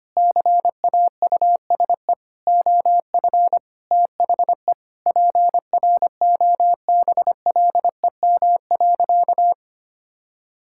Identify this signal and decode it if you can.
Morse code, heard as a keyed tone, about 25 wpm, 700 Hz. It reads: CAUSE OF THE PROBLEM.
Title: Sentences from Top 1000 Words - Review Part 5 - 25wpm